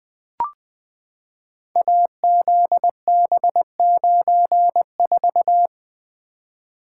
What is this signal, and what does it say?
Morse code, heard as a keyed tone, about 20 wpm, 700 Hz. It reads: AZB94
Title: Code Groups - 5 Letter:Numbers - 20wpm